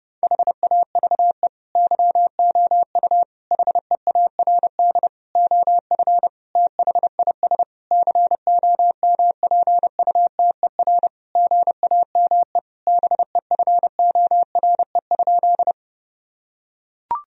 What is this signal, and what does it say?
Morse code, heard as a keyed tone, about 32 wpm, 700 Hz. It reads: HAVE YOU HEARD OF THIS COMPUTER GAME BEFORE?
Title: Sentences from Top 1000 Words - Review Part 3 - 30wpm